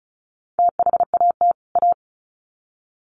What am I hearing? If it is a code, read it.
Morse code, heard as a keyed tone, about 34 wpm, 700 Hz. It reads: THAT A
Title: Sets of 2 Words 35wpm